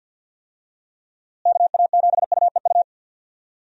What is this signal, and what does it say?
Morse code, heard as a keyed tone, about 52 wpm, 700 Hz. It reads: KA7UEU
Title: USA Call Signs - 6 Characters - Repeated 2x - 50wpm